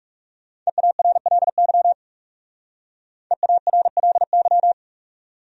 Morse code, heard as a keyed tone, about 36 wpm, 700 Hz. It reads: EARLY EARLY
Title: Speed-Racing Top 500 Words 30wpm